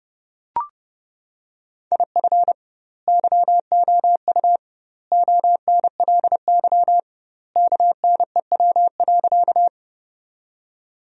Morse code, heard as a keyed tone, about 30 wpm, 700 Hz. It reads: IF YOU ONLY KNEW.
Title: 2x Word-Spacing - Sentences from Top 400 Words - 30wpm